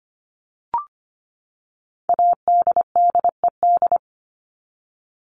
Morse code, heard as a keyed tone, about 25 wpm, 700 Hz. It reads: ADDED
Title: ED Suffix - Top 100 Words with Base Words 25wpm